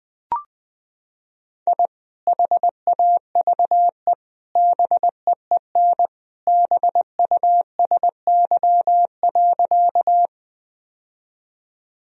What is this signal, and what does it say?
Morse code, heard as a keyed tone, about 21 wpm, 700 Hz. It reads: I HAVE BEEN BUSY.
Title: Sentences from Top 1000 Words - Review Part 3 - 20wpm